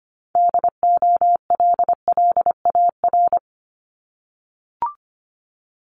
Morse code, heard as a keyed tone, about 25 wpm, 700 Hz. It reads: DOLLAR
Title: Multi-letter Phonemes 25wpm